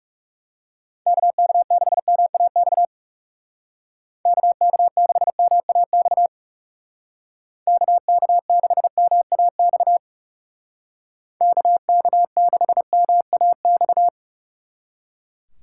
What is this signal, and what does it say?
Morse code, heard as a keyed tone, about 39 wpm, 700 Hz. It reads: KK6MAX KK6MAX KK6MAX KK6MAX